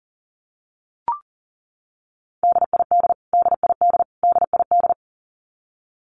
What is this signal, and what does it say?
Morse code, heard as a keyed tone, about 40 wpm, 700 Hz. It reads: DID DID DID
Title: Top 300 Words Repeated 40wpm